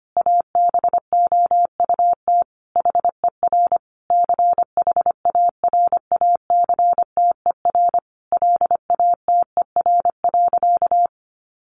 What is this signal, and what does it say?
Morse code, heard as a keyed tone, about 25 wpm, 700 Hz. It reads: ABOUT HER CHARACTER LATER.
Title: Sentences from Top 1200 Words 25wpm